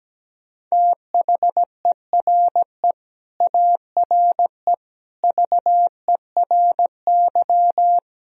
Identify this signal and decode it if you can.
Morse code, heard as a keyed tone, about 17 wpm, 700 Hz. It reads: THERE ARE VERY
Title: Sentences from Top 1000 Words - Review Part 1 - 17wpm